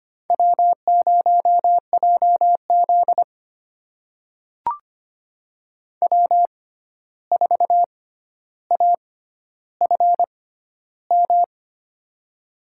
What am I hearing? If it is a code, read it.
Morse code, heard as a keyed tone, about 25 wpm, spaced as for 12 wpm, 700 Hz. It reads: W0JZ W4AFM
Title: USA Call Signs - 6x Character Spacing - 25wpm